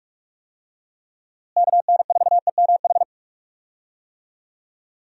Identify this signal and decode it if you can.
Morse code, heard as a keyed tone, about 47 wpm, 700 Hz. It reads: KN4EMH
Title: USA Call Signs - 6 Characters - Repeated 3x - 45wpm